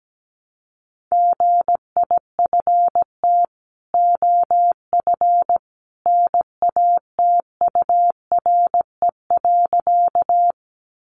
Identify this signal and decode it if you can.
Morse code, heard as a keyed tone, about 17 wpm, 700 Hz. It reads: GIFT OF NATURE.
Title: Sentences from Top 900 Words 17wpm